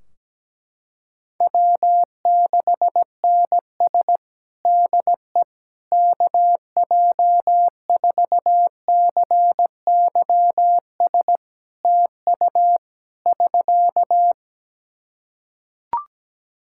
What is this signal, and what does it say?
Morse code, heard as a keyed tone, about 17 wpm, 700 Hz. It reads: W6NS DE KJ4CYS TU <SK>
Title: Practice QSO - Exchange 4 - Ending - 17wpm